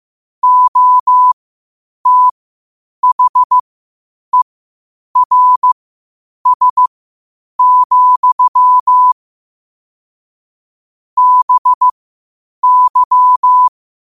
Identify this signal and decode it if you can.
Morse code, heard as a keyed tone, about 15 wpm, spaced as for 8 wpm, 1000 Hz. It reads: OTHERS, BY
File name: qotdfiles / 8wpm